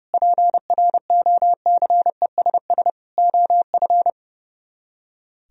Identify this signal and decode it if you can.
Morse code, heard as a keyed tone, about 37 wpm, 700 Hz. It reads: PROCESS OF
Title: Speed-Racing - Sets of 4 Words 30wpm